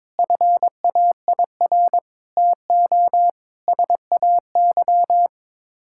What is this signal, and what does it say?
Morse code, heard as a keyed tone, about 22 wpm, 700 Hz. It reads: FAIR TO SAY
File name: practice / 22wpm / Sets of 6 Words - 22wpm